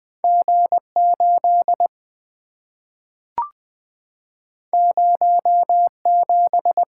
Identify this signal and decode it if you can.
Morse code, heard as a keyed tone, about 20 wpm, 700 Hz. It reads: G 8 0 7
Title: Two Letter-Number Combinations - 20wpm